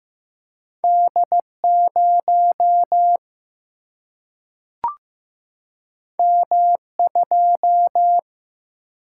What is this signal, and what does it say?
Morse code, heard as a keyed tone, about 15 wpm, 700 Hz. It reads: D 0 M 2
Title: Single-Letter Single-Number 15wpm